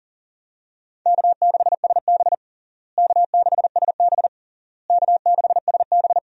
Morse code, heard as a keyed tone, about 40 wpm, 700 Hz. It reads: K6SB K6SB K6SB
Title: USA Call Signs - 4 Characters - Repeated 3x - 40wpm